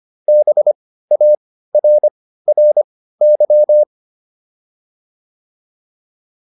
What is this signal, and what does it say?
Morse code, heard as a keyed tone, about 25 wpm, spaced as for 15 wpm, 600 Hz. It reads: BARRY